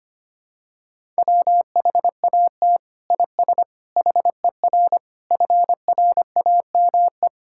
Morse code, heard as a keyed tone, about 25 wpm, 700 Hz. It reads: WHAT IS HER FRAME
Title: Sentences from Top 1000 Words - Review Part 6 - 25wpm